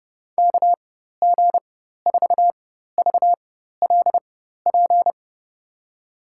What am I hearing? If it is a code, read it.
Morse code, heard as a keyed tone, about 30 wpm, spaced as for 14 wpm, 700 Hz. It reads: KG4VLP
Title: USA Call Signs - 6 Characters - 4x Character Spacing - 30wpm